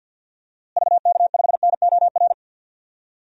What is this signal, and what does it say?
Morse code, heard as a keyed tone, about 50 wpm, 700 Hz. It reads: VK5NOR
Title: DXCC-100 Call Signs - 6 Characters - Repeated 3x - 50wpm